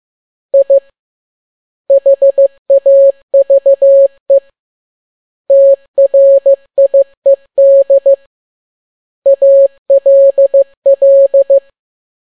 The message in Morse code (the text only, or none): I HAVE TRIED ALL